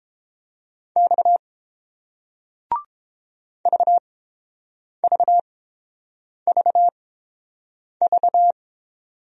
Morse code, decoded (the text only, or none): X VVVV